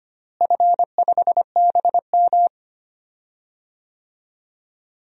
Morse code, heard as a keyed tone, about 25 wpm, 700 Hz. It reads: F5BM
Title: DXCC-100 Call Signs - 4 Characters - Repeated 3x - 25wpm